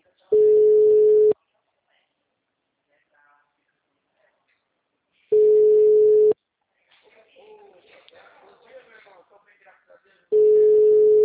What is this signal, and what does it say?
Ringback tone (Europe, 425 Hz), with speech.